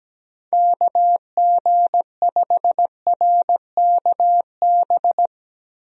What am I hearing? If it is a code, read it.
Morse code, heard as a keyed tone, about 17 wpm, 700 Hz. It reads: KG5RKB
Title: USA Call Signs - 6 Characters - 17wpm